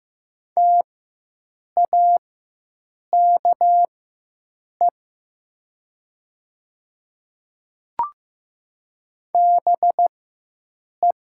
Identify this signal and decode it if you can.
Morse code, heard as a keyed tone, about 15 wpm, spaced as for 7 wpm, 700 Hz. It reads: TAKE BE
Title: 4x Character Spacing - Top 300 Words - 15wpm